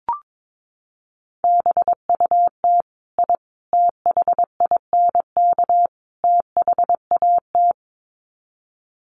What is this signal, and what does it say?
Morse code, heard as a keyed tone, about 22 wpm, 700 Hz. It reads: BUT I THINK THAT
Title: Sets of 4 Words 22wpm